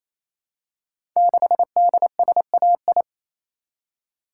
Morse code, heard as a keyed tone, about 27 wpm, 700 Hz. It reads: 6DSAI
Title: Code Groups - 5 Letter:Numbers - 28wpm